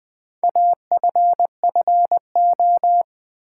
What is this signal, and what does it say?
Morse code, heard as a keyed tone, about 20 wpm, 700 Hz. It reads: AFFO
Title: Build A Word - 4 Letters At A Time - 10-Letter Words - 20wpm